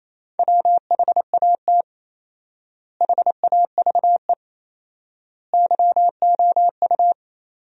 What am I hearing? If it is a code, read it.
Morse code, heard as a keyed tone, about 28 wpm, 700 Hz. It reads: WHAT HAVE YOU
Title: 4x Word-Spacing - Sentences from Top 100 Words - 28wpm